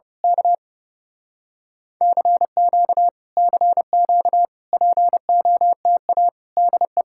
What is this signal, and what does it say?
Morse code, heard as a keyed tone, about 29 wpm, 700 Hz. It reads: K CQ CQ POTA DE
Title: Speed-Racing Common QSO Phrases - 30wpm